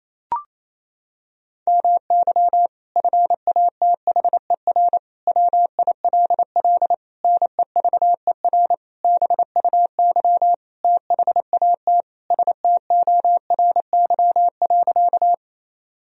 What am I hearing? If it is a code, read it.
Morse code, heard as a keyed tone, about 28 wpm, 700 Hz. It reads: MY FATHER WILL NEVER BUY THAT STORY.
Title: Sentences from Top 1000 Words - Review Part 4 - 28wpm